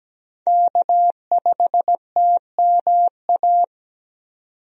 Morse code, heard as a keyed tone, about 17 wpm, 700 Hz. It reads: K5TMA
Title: USA Call Signs - 5 Characters - 17wpm